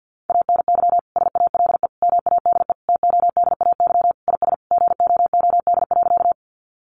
Morse code, heard as a keyed tone, about 50 wpm, 700 Hz. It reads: ANY SALE MADE TODAY IS GOOD.
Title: Sentences from Top 1400 Words 50wpm